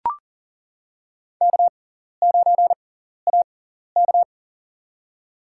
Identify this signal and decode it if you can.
Morse code, heard as a keyed tone, about 40 wpm, spaced as for 14 wpm, 700 Hz. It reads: K9AK
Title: USA Call Signs - 4 Characters - 6x Character Spacing - 40wpm